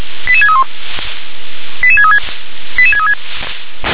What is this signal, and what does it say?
Five-tone selective call (ZVEI), heard as quick runs of five tones, tones of about 70 ms. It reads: 88631 88636 88636